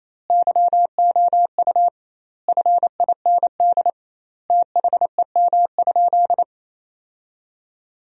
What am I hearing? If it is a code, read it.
Morse code, heard as a keyed tone, about 28 wpm, 700 Hz. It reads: YOU FIND THEM?
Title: 2x Word-Spacing - Sentences from Top 200 Words - 28wpm